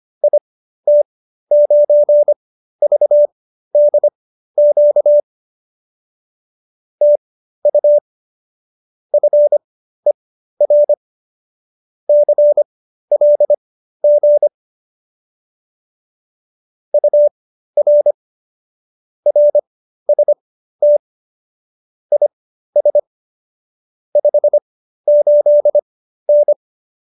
Morse code, heard as a keyed tone, about 25 wpm, spaced as for 13 wpm, 590 Hz. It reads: IT9VDQ TU FER CLG UR RST IS 58N